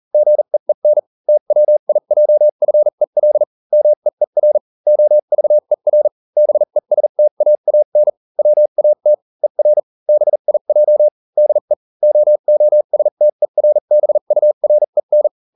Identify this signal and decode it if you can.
Morse code, heard as a keyed tone, about 40 wpm, spaced as for 35 wpm, 600 Hz. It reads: GEEN TWIJFEL MEER OVER BESTAAN WAT ER BIJ DE OOSTERBUREN